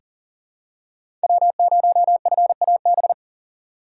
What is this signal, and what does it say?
Morse code, heard as a keyed tone, about 40 wpm, 700 Hz. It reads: W0FAB
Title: USA Call Signs - 6x Character Spacing - 40wpm